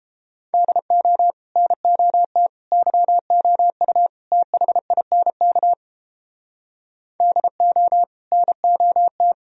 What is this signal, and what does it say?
Morse code, heard as a keyed tone, about 29 wpm, 700 Hz. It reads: DO NOT YOU THINK DO NOT